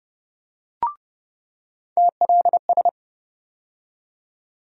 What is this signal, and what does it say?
Morse code, heard as a keyed tone, about 30 wpm, 700 Hz. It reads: TLS
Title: USA Call Sign Postfixes - 3 Letters 30wpm